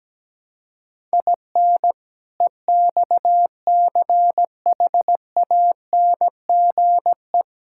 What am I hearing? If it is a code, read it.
Morse code, heard as a keyed tone, about 17 wpm, 700 Hz. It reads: IN EXCHANGE
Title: Sentences from Top 1000 Words - Review Part 6 - 17wpm